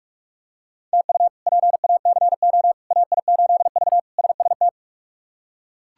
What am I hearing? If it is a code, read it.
Morse code, heard as a keyed tone, about 42 wpm, 700 Hz. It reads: TU PACO AI8V SST
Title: K1USN SST QSO Practice - 45wpm